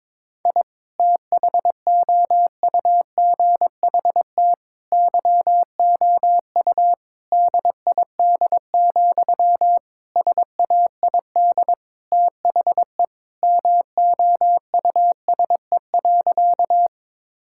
Morse code, heard as a keyed tone, about 22 wpm, 700 Hz. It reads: I THOUGHT YOU DID, SAID THE MOUSE.
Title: Sentences from Top 1000 Words - Review Part 5 - 22wpm